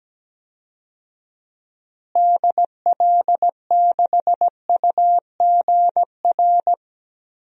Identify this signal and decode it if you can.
Morse code, heard as a keyed tone, about 17 wpm, 700 Hz. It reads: DL6UGR
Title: DXCC-100 Call Signs - 6 Characters - Repeated 3x - 17wpm